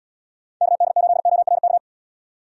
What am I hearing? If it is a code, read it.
Morse code, heard as a keyed tone, about 72 wpm, 700 Hz. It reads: KD8ORK